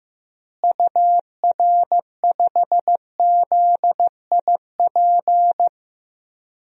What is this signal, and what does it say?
Morse code, heard as a keyed tone, about 15 wpm, 700 Hz. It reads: UR5ZIP
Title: DXCC-100 Call Signs - 6 Characters - Repeated 3x - 15wpm